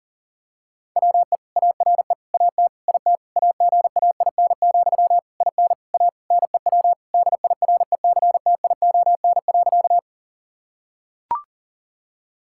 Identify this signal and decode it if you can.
Morse code, heard as a keyed tone, about 37 wpm, 700 Hz. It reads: WE ARE AT IT AGAIN, IN A NEW DIRECTION.
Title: Sentences from Top 1000 Words - Review Part 5 - 40wpm